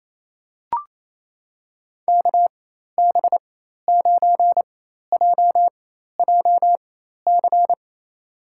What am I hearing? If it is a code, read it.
Morse code, heard as a keyed tone, about 27 wpm, spaced as for 13 wpm, 700 Hz. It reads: KB9JJC
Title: USA Call Signs - 4x Character Spacing - 28wpm